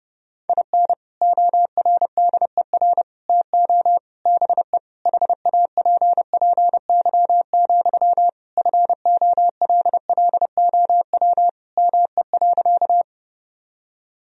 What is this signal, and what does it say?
Morse code, heard as a keyed tone, about 32 wpm, 700 Hz. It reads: IN ORDER TO BE HAPPY, FOLLOW ME.
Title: Sentences from Top 1000 Words - Review Part 2 - 30wpm